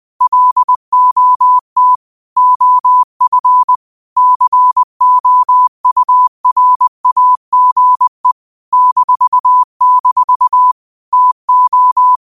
Morse code, heard as a keyed tone, about 20 wpm, 1000 Hz. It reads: LOT OF COURAGE -- TO
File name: qotdfiles / 20wpm